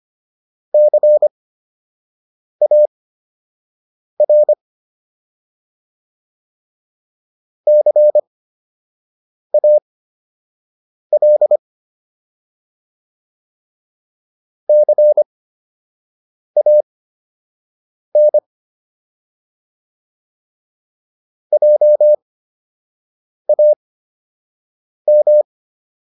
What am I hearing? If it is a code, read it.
Morse code, heard as a keyed tone, about 25 wpm, spaced as for 6 wpm, 600 Hz. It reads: CAR CAL CAN JAM